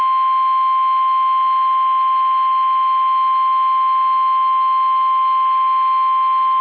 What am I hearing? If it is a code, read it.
Emergency Alert System (SAME) tones (the 1050 Hz attention tone). It is heard from 0.4 to 6.4 s.
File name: eas / KUSO / KUSO